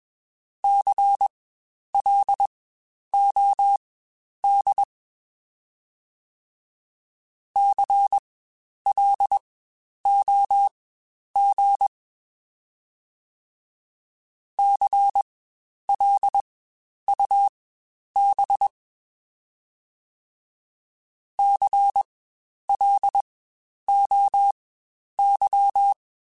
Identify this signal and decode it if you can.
Morse code, heard as a keyed tone, about 21 wpm, spaced as for 10 wpm, 800 Hz. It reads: CLOD CLOG CLUB CLOY
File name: Spaced Alphas / Spaced Alphas mp3s / Cs sp @20